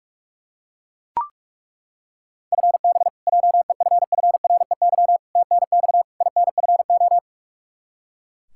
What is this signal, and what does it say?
Morse code, heard as a keyed tone, about 41 wpm, 700 Hz. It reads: FB JEFFREY TNX INFO
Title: K1USN SST QSO Practice 45wpm